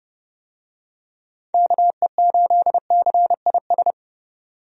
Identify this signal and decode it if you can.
Morse code, heard as a keyed tone, about 30 wpm, 700 Hz. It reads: KE8CIS
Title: USA Call Signs - 6 Characters - 30wpm